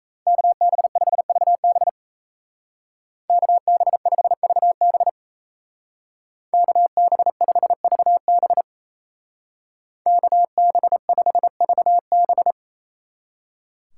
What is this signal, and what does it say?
Morse code, heard as a keyed tone, about 34 wpm, 700 Hz. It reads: KB5VB KB5VB KB5VB KB5VB
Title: Speed Racing - USA 2x2 Call Signs 28wpm